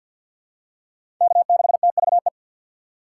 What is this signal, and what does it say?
Morse code, heard as a keyed tone, about 52 wpm, 700 Hz. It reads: K6TVE
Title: USA Call Signs - 5 Characters - 4x Character Spacing - 50wpm